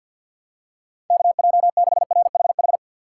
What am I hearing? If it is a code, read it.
Morse code, heard as a keyed tone, about 52 wpm, 700 Hz. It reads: KJ6RHH
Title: USA Call Signs - 6 Characters - 6x Character Spacing - 50wpm